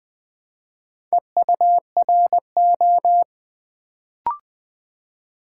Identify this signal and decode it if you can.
Morse code, heard as a keyed tone, about 20 wpm, 690 Hz. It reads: EURO
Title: Build A Word - 2 Letters At A Time - 4-Letter Words - 20wpm